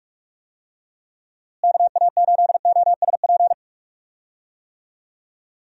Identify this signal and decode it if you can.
Morse code, heard as a keyed tone, about 46 wpm, 700 Hz. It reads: KA8OSP
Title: USA Call Signs - 6 Characters - Repeated 2x - 45wpm